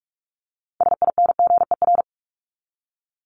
Morse code, heard as a keyed tone, about 44 wpm, 700 Hz. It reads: SINGER
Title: Top 1501-1750 Words 45wpm